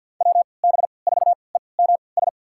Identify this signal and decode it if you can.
Morse code, heard as a keyed tone, about 48 wpm, spaced as for 28 wpm, 700 Hz. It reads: WB4EMS
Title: USA Call Signs - 3x Character Spacing - 50wpm